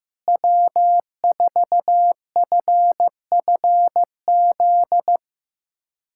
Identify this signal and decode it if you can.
Morse code, heard as a keyed tone, about 15 wpm, 700 Hz. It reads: W4FFZ
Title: USA Callsign DE Callsign - 5 Characters - 15wpm